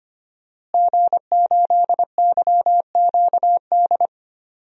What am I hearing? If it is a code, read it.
Morse code, heard as a keyed tone, about 25 wpm, 700 Hz. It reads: G8YQD
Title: DXCC-100 Call Signs - 5 Characters - Repeated 2x - 25wpm